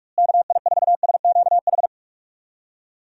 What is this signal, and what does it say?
Morse code, heard as a keyed tone, about 46 wpm, 700 Hz. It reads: KI4SQH
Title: USA Call Signs - 6 Characters - Repeated 3x - 45wpm